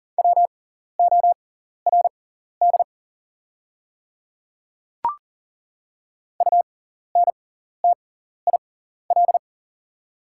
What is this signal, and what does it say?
Morse code, heard as a keyed tone, about 40 wpm, spaced as for 14 wpm, 690 Hz. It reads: WORD UNTIL
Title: 6x Character Spacing - Top 300 Words - 40wpm